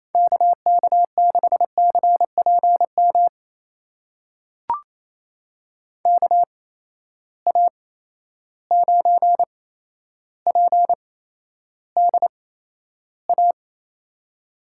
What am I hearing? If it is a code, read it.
Morse code, heard as a keyed tone, about 28 wpm, spaced as for 12 wpm, 700 Hz. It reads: KK6CPM KA9PDA
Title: USA Call Signs - 6 Characters - 8x Character Spacing - 28wpm